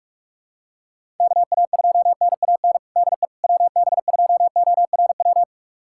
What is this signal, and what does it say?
Morse code, heard as a keyed tone, about 46 wpm, 700 Hz. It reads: KA2NAN DE WB2ORW
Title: USA Callsign DE Callsign 45wpm